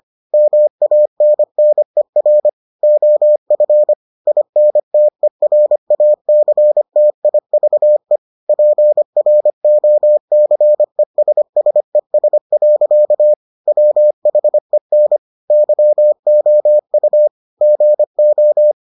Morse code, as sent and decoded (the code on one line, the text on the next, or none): -- .- -. -. .-. / --- ..-. / .. -. - . .-. .- -.-. - .. ...- . / .--. .-. --- -.-. ... ... ... .-.-.- / .-- .... -. / -.-- --- ..- / --. ---
MANNER OF INTERACTIVE PROCESSES. WHEN YOU GO